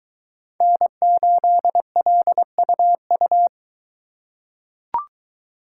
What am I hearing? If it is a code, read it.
Morse code, heard as a keyed tone, about 22 wpm, 700 Hz. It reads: N8LUU